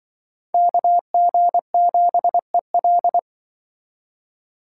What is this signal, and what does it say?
Morse code, heard as a keyed tone, about 24 wpm, 700 Hz. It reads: KG7EL
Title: Speed Racing - USA Call Signs 20wpm